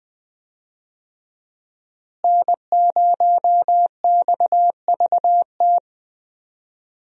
Morse code, heard as a keyed tone, about 20 wpm, 700 Hz. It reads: N0XVT
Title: USA Call Signs - 5 Characters - Repeated 2x - 20wpm